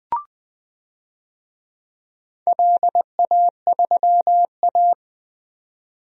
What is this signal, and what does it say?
Morse code, heard as a keyed tone, about 20 wpm, 700 Hz. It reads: LA3A
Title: DXCC-100 Call Signs - 4 Characters - 20wpm